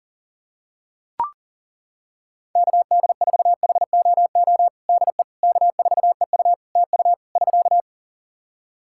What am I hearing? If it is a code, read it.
Morse code, heard as a keyed tone, about 40 wpm, 700 Hz. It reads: KD4HOO DE K4EU TU <SK>
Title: Practice QSO - Exchange 4 - Ending - 40wpm